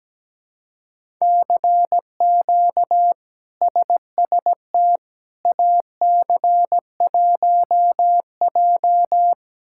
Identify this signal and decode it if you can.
Morse code, heard as a keyed tone, about 17 wpm, 700 Hz. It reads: CQ SST AC1J
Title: K1USN SST QSO Practice - 17wpm